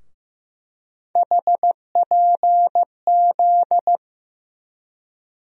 Morse code, heard as a keyed tone, about 16 wpm, 700 Hz. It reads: HPZ
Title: USA Call Sign Postfixes - 3 Letters 15wpm